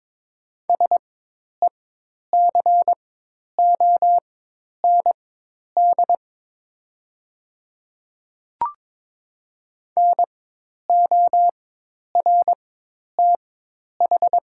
Morse code, heard as a keyed tone, about 22 wpm, spaced as for 10 wpm, 700 Hz. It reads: SECOND NORTH
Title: 4x Character Spacing - Top 500 Words - 22wpm